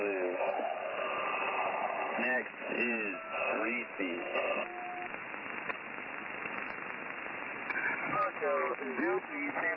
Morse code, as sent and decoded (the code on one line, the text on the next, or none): none